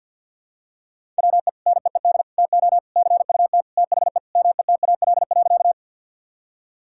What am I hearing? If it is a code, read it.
Morse code, heard as a keyed tone, about 47 wpm, 700 Hz. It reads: WE NEED TO CUT THE METAL.
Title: Sentences from Top 1000 Words - Review Part 6 - 50wpm